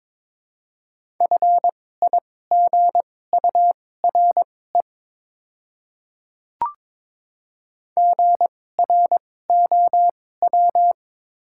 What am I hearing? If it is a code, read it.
Morse code, heard as a keyed tone, about 23 wpm, spaced as for 16 wpm, 700 Hz. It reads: FIGURE GROW
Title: 2x Character Spacing - Top 500 Words - 22wpm